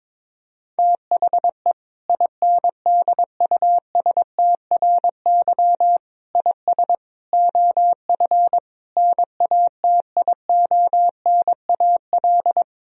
Morse code, heard as a keyed tone, about 22 wpm, 700 Hz. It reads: THE INDUSTRY IS OF NATIONAL
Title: Sentences from Top 1000 Words - Review Part 4 - 22wpm